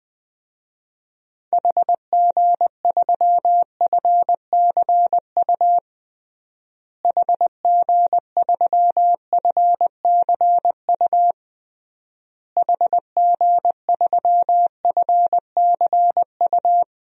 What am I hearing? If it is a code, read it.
Morse code, heard as a keyed tone, about 21 wpm, 700 Hz. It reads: HG3FCU HG3FCU HG3FCU
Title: DXCC-100 Call Signs - 6 Characters - Repeated 3x - 20wpm